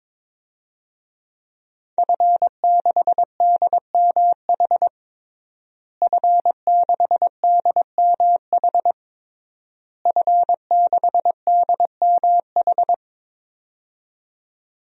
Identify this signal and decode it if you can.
Morse code, heard as a keyed tone, about 22 wpm, 700 Hz. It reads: F6DMH F6DMH F6DMH
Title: DXCC-100 Call Signs - 5 Characters - Repeated 3x - 22wpm